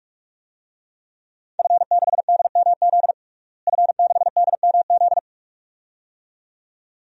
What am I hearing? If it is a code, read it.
Morse code, heard as a keyed tone, about 42 wpm, 700 Hz. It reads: F6DMZ F6DMZ
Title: DXCC-100 Call Signs - 5 Characters - Repeated 2x - 45wpm